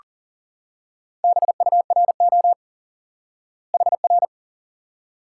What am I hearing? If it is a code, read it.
Morse code, heard as a keyed tone, about 40 wpm, 700 Hz. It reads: BURO HR
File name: IQE Course 3 - Lesson 89 - YAGI - Two Elements - 40wpm